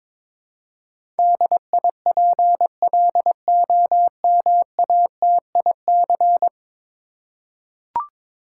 Morse code, heard as a keyed tone, about 22 wpm, 700 Hz. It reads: DIPLOMATIC
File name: practice / 22wpm / 10 Letter Words - First 3 Letters Given - 22wpm